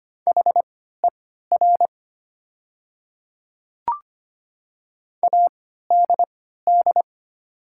Morse code, heard as a keyed tone, about 25 wpm, spaced as for 14 wpm, 700 Hz. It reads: HER ADD
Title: 3x Character Spacing - Top 200 Words - 25wpm